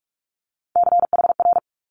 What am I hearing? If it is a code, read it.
Morse code, heard as a keyed tone, about 46 wpm, 700 Hz. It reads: CHR
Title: Multi-letter Phonemes Quick Fire 45wpm